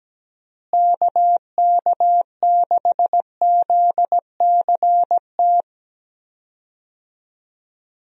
Morse code, heard as a keyed tone, about 17 wpm, 700 Hz. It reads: KK6ZCT
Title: USA Call Signs - 6 Characters - Repeated 3x - 17wpm